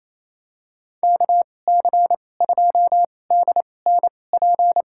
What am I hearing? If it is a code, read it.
Morse code, heard as a keyed tone, about 28 wpm, spaced as for 20 wpm, 700 Hz. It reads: KC2DNP
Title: USA Call Signs - 2x Character Spacing - 28wpm